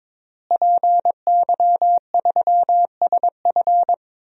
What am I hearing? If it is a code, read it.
Morse code, heard as a keyed tone, about 22 wpm, 700 Hz. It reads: PY3SF
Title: DXCC-100 Call Signs - 5 Characters - Repeated 2x - 22wpm